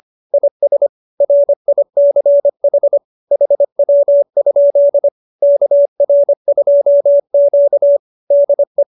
Morse code, heard as a keyed tone, about 25 wpm, 570 Hz. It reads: IS RICH HW? KR2Q DE